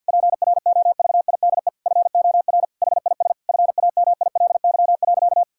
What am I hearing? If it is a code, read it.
Morse code, heard as a keyed tone, about 51 wpm, 690 Hz. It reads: PROVIDE FOR HIS FAMILY.
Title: Sentences from Top 1000 Words - Review Part 5 - 50wpm